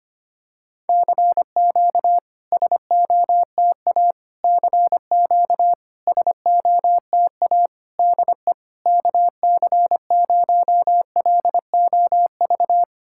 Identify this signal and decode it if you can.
Morse code, heard as a keyed tone, about 25 wpm, 700 Hz. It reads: CQ SOTA CQ SOTA DE KC0LOV
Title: SOTA QSO Practice 25wpm